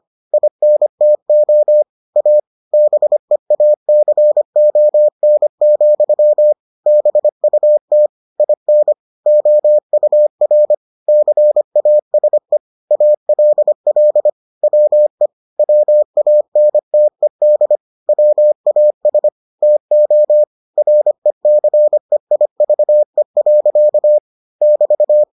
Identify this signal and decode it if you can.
Morse code, heard as a keyed tone, about 25 wpm, 600 Hz. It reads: INTO A BEACON, BUT IN OUR CASE ALL WE WANTED WAS TO RECEIVE. <BT>